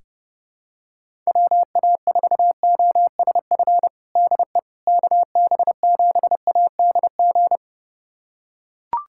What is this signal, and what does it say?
Morse code, heard as a keyed tone, about 32 wpm, 700 Hz. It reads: WA4OSF DE KB7ADG
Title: USA Callsign DE Callsign - 6 Characters - 30wpm